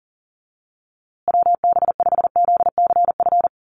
Morse code, heard as a keyed tone, about 40 wpm, 700 Hz. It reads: WB5ZCF
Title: USA Call Signs Repeated 3x 40wpm